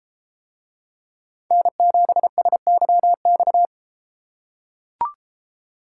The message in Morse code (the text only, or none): N7SYX